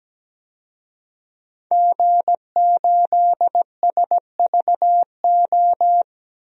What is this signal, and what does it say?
Morse code, heard as a keyed tone, about 17 wpm, 700 Hz. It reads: G8SVO
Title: DXCC-100 Call Signs - 5 Characters - 17wpm